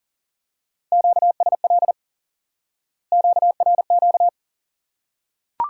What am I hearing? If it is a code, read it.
Morse code, heard as a keyed tone, about 40 wpm, 700 Hz. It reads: QSL QRQ
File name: IQE Course 3 - Lesson 89 - YAGI - Two Elements - 40wpm